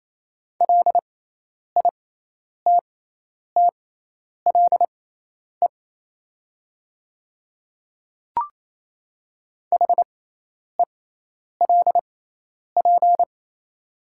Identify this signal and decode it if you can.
Morse code, heard as a keyed tone, about 27 wpm, spaced as for 10 wpm, 690 Hz. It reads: LITTLE HELP
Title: 6x Character Spacing - Top 500 Words - 28wpm